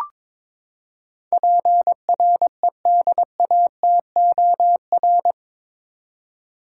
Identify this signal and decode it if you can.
Morse code, heard as a keyed tone, about 22 wpm, 700 Hz. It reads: PREDATOR
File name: practice / 22wpm / Eight Letter Words - 22wpm